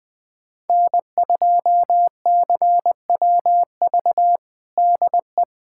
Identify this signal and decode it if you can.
Morse code, heard as a keyed tone, about 21 wpm, 700 Hz. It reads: N2CWV DE
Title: USA Callsign DE Callsign - 5 Characters - 20wpm